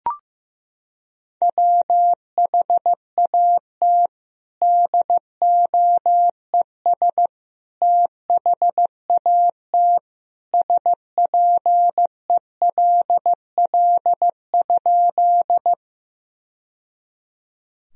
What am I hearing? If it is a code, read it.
Morse code, heard as a keyed tone, about 15 wpm, 700 Hz. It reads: WHAT DOES THAT SPELL?
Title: Sentences from Top 1000 Words - Review Part 1 - 15wpm